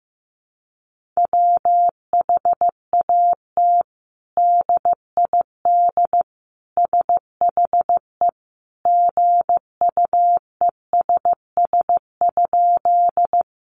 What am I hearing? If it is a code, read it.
Morse code, heard as a keyed tone, about 15 wpm, 700 Hz. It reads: WHAT DID SHE GUESS?
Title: Sentences from Top 1500 Words 15wpm